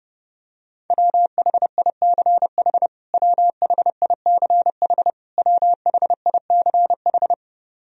Morse code, heard as a keyed tone, about 30 wpm, 700 Hz. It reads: WHICH WHICH WHICH
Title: Top 100 Words Repeated 30wpm